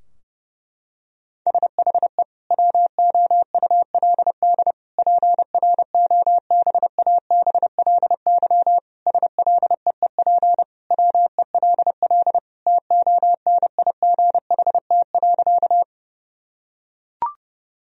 Morse code, heard as a keyed tone, about 32 wpm, 700 Hz. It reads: SHE WOULD PROBABLY SLEEP WELL TONIGHT.
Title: Sentences from Top 1000 Words - Review Part 4 - 30wpm